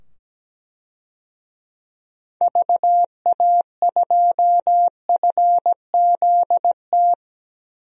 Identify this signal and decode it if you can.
Morse code, heard as a keyed tone, about 17 wpm, 700 Hz. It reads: VA2FZT
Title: DXCC-100 Call Signs - 6 Characters - 17wpm